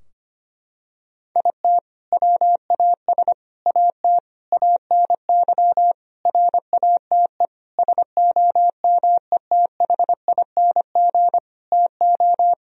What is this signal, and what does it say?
Morse code, heard as a keyed tone, about 25 wpm, 700 Hz. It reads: IT WAS AT ANY RATE SOMETHING TO
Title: Sentences from Top 1000 Words - Review Part 4 - 25wpm